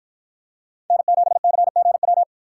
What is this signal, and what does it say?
Morse code, heard as a keyed tone, about 53 wpm, 700 Hz. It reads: N7CGW